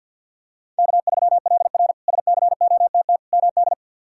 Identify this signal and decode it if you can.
Morse code, heard as a keyed tone, about 48 wpm, 700 Hz. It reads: K3LR SCOTT MD
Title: K1USN SST QSO Practice - 50wpm